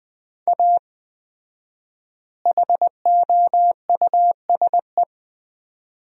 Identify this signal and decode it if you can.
Morse code, heard as a keyed tone, about 20 wpm, 700 Hz. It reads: A HOUSE
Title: 4x Word-Spacing - Sentences from Top 400 Words - 20wpm